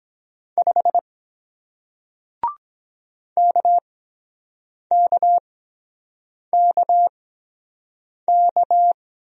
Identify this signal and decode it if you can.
Morse code, heard as a keyed tone, about 23 wpm, 700 Hz. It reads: 5 K K K K